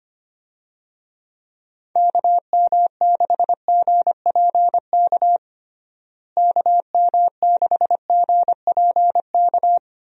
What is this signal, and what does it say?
Morse code, heard as a keyed tone, about 25 wpm, 700 Hz. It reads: KM6GPK KM6GPK
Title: USA Call Signs - 6 Characters - Repeated 2x - 25wpm